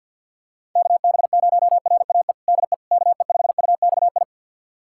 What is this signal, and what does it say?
Morse code, heard as a keyed tone, about 50 wpm, 700 Hz. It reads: KB0RAE DE KE5UXI
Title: USA Callsign DE Callsign 50wpm